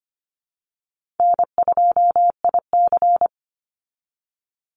Morse code, heard as a keyed tone, about 25 wpm, 700 Hz. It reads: N2IC
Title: Call Signs Repeated 3x - 25wpm